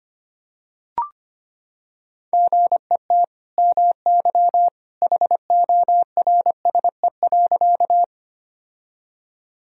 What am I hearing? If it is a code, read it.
Morse code, heard as a keyed tone, about 25 wpm, 700 Hz. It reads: GET MY HORSE.
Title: Sentences from Top 1000 Words - Review Part 1 - 25wpm